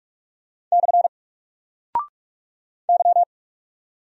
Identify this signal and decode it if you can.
Morse code, heard as a keyed tone, about 44 wpm, 700 Hz. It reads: / Y
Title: Single Character (Letter:Number:Punctuation:Prosign) Alt 45wpm